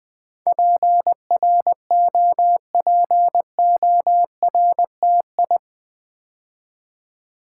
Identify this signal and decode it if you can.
Morse code, heard as a keyed tone, about 20 wpm, 700 Hz. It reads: PROPORTI
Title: Build A Word - 4 Letters At A Time - 10-Letter Words - 20wpm